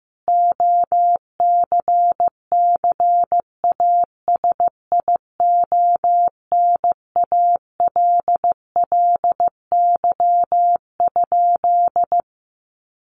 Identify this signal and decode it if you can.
Morse code, heard as a keyed tone, about 14 wpm, 700 Hz. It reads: OCCASIONALLY?
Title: Sentences from Top 1300 Words 15wpm